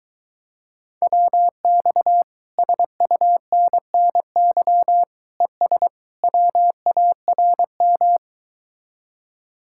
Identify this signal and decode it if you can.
Morse code, heard as a keyed tone, about 23 wpm, 700 Hz. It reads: WX SUNNY ES WARM